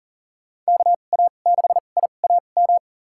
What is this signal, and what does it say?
Morse code, heard as a keyed tone, about 38 wpm, spaced as for 28 wpm, 700 Hz. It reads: KA6IAM
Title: USA Call Signs - 2x Character Spacing - 40wpm